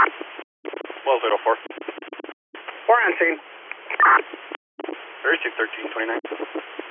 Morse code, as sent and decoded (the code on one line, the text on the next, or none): none